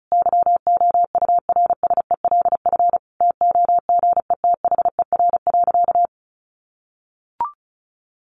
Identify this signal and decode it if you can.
Morse code, heard as a keyed tone, about 34 wpm, 700 Hz. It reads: YOURSELF TOGETHER.
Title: Sentences from Top 600 Words 35wpm